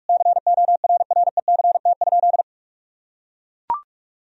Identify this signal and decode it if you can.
Morse code, heard as a keyed tone, about 43 wpm, 700 Hz. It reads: CORRECT?
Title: Sentences from Top 1000 Words - Review Part 2 - 45wpm